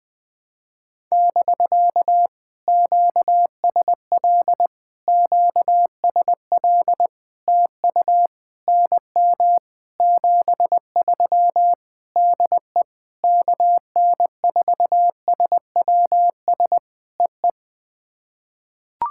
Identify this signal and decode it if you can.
Morse code, heard as a keyed tone, about 20 wpm, 700 Hz. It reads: <BK> QSL QSL TU NM 73 DE KN4SWS EE
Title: POTA QSO Practice - 20wpm